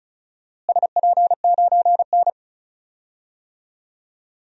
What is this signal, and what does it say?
Morse code, heard as a keyed tone, about 36 wpm, 700 Hz. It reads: SP9N
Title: DXCC-100 Call Signs - 4 Characters - Repeated 3x - 35wpm